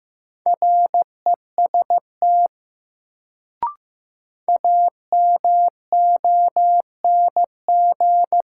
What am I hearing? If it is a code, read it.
Morse code, heard as a keyed tone, about 15 wpm, 700 Hz. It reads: REST AMONG
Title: Sentences from Top 500 Words - New Word Drill - 15wpm